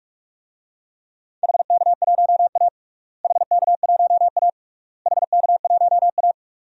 Morse code, heard as a keyed tone, about 46 wpm, 700 Hz. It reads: HK1A HK1A HK1A
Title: DXCC-100 Call Signs - 4 Characters - Repeated 3x - 45wpm